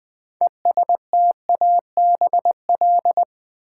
Morse code, heard as a keyed tone, about 20 wpm, 700 Hz. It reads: ESTABL
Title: Build A Word - 3 Letters At A Time - 9-Letter Words - 20wpm